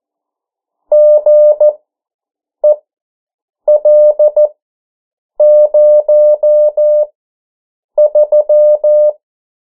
Morse code, heard as a keyed tone, about 14 wpm, spaced as for 7 wpm, 600 Hz. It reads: GEL03
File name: morse mp3 / 09 urok 8